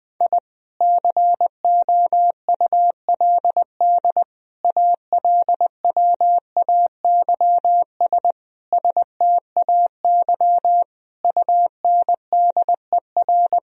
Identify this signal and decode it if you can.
Morse code, heard as a keyed tone, about 20 wpm, 700 Hz. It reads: I COULD ALWAYS STAY UNDER
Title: Sentences from Top 1000 Words - Review Part 1 - 20wpm